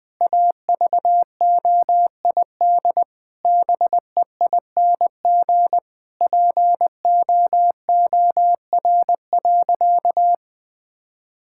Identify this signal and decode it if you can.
Morse code, heard as a keyed tone, about 20 wpm, 700 Hz. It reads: AVOID BEING POOR.
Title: Sentences from Top 1000 Words - Review Part 6 - 20wpm